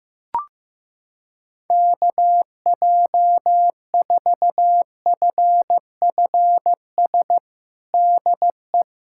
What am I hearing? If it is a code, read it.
Morse code, heard as a keyed tone, about 15 wpm, 700 Hz. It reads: KJ4FFS DE
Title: USA Callsign DE Callsign - 6 Characters - 15wpm